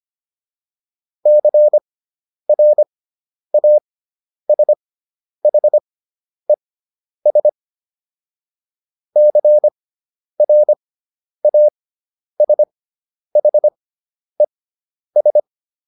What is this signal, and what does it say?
Morse code, heard as a keyed tone, about 25 wpm, spaced as for 10 wpm, 600 Hz. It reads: CRASHES CRASHES